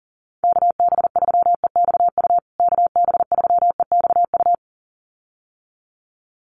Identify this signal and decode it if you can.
Morse code, heard as a keyed tone, about 40 wpm, 700 Hz. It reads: KB3EXU KB3EXU
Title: USA Call Signs Repeated 2x 40wpm